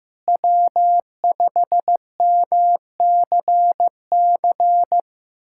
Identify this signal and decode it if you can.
Morse code, heard as a keyed tone, about 16 wpm, 700 Hz. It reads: W5MCC
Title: USA Callsign DE Callsign - 5 Characters - 15wpm